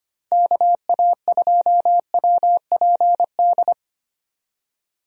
Morse code, heard as a keyed tone, about 25 wpm, 700 Hz. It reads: KA2WPD
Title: USA Call Signs - 6 Characters - 25wpm